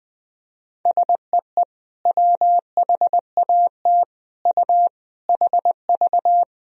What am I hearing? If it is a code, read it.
Morse code, heard as a keyed tone, about 21 wpm, 700 Hz. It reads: SEE WHAT U HV
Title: Sentences from Top 500 Words and Abbreviations 20wpm